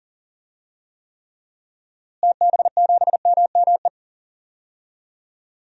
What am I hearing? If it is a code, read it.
Morse code, heard as a keyed tone, about 40 wpm, 700 Hz. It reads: TB7MME
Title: DXCC-100 Call Signs - 6 Characters - Repeated 3x - 40wpm